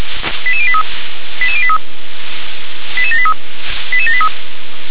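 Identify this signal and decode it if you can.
Five-tone selective call (ZVEI), heard as quick runs of five tones, tones of about 70 ms. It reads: 88083 88083 88783 88783
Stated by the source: ZVEI five-tone selective call received on an SDR.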